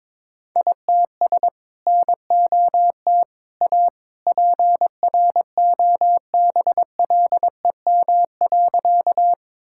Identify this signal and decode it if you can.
Morse code, heard as a keyed tone, about 22 wpm, 700 Hz. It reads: ITS NOT A PROBLEM.
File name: practice / 22wpm / Sentences with 2-Word Phrases - 22wpm